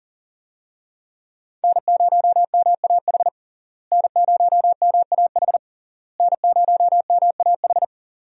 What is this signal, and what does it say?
Morse code, heard as a keyed tone, about 40 wpm, 690 Hz. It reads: N0MAH N0MAH N0MAH
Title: USA Call Signs - 5 Characters - Repeated 3x - 40wpm